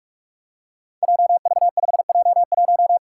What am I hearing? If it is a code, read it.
Morse code, heard as a keyed tone, about 44 wpm, 700 Hz. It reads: JV5J1
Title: Five Challenging Letter-Number Combinations 45wpm